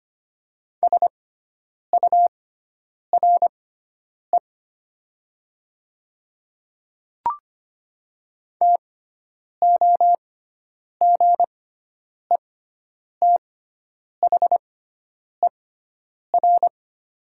Morse code, heard as a keyed tone, about 25 wpm, spaced as for 9 wpm, 700 Hz. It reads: SURE TOGETHER
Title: 6x Character Spacing - Top 300 Words - 25wpm